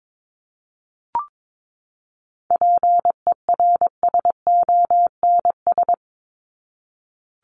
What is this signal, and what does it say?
Morse code, heard as a keyed tone, about 22 wpm, 700 Hz. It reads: PERSONS
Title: ES:S Suffix - Top 100 Words with Base Words 22wpm